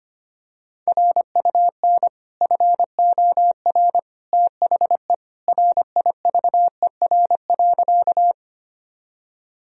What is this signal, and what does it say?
Morse code, heard as a keyed tone, about 25 wpm, 700 Hz. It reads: RUN FOR THE RIVER.
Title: Sentences from Top 1000 Words - Review Part 1 - 25wpm